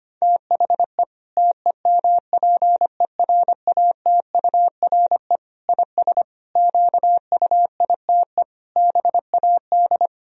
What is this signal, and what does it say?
Morse code, heard as a keyed tone, about 25 wpm, 700 Hz. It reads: THE TEMPERATURE IS QUITE BAD
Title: Sentences from Top 1000 Words - Review Part 4 - 25wpm